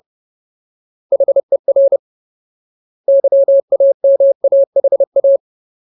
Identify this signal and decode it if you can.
Morse code, heard as a keyed tone, about 30 wpm, 550 Hz. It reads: HER YAMAHA
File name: Ultimate Word List / 30WPM Ultimate